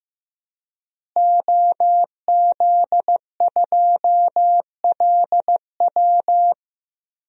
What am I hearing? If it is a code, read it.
Morse code, heard as a keyed tone, about 15 wpm, 700 Hz. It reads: OZ2LW
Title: DXCC-100 Call Signs - 5 Characters - Repeated 3x - 15wpm